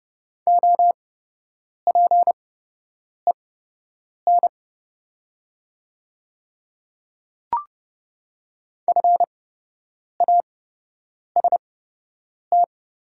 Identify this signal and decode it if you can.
Morse code, heard as a keyed tone, about 30 wpm, spaced as for 8 wpm, 700 Hz. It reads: OPEN FAST
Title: 8x Character Spacing - Top 500 Words - 30wpm